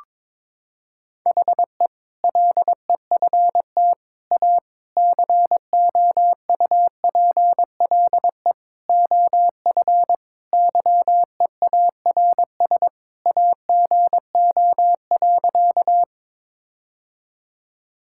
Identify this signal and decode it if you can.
Morse code, heard as a keyed tone, about 22 wpm, 700 Hz. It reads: HE LEFT A COUPLE OF YEARS AGO.
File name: practice / 22wpm / Sentences with 4-Word Phrases - 22wpm